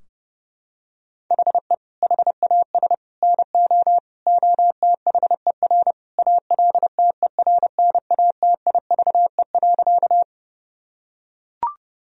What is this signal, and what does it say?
Morse code, heard as a keyed tone, about 30 wpm, 700 Hz. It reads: HE HAS NO OTHER ALTERNATIVE.
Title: Sentences from Top 1000 Words - Review Part 4 - 30wpm